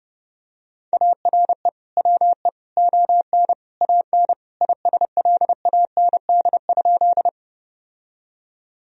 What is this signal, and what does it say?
Morse code, heard as a keyed tone, about 32 wpm, 700 Hz. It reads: ARE WE ON AN ISLAND?
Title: Sentences from Top 1000 Words - Review Part 3 - 30wpm